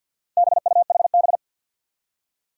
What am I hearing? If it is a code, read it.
Morse code, heard as a keyed tone, about 53 wpm, 700 Hz. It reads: BUHB
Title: Four Challenging Letter-Number Combinations 50wpm